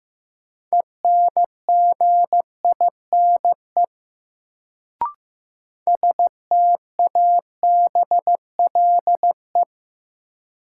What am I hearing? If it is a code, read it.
Morse code, heard as a keyed tone, about 15 wpm, 700 Hz. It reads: ENGINE STABLE
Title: Sentences from Top 1500 Words - New Word Drill Alt - 15wpm